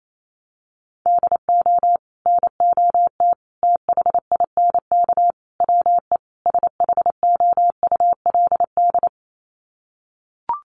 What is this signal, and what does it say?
Morse code, heard as a keyed tone, about 28 wpm, 700 Hz. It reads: DO NOT THINK WE SHOULD
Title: Sets of 5 Words Encore 28wpm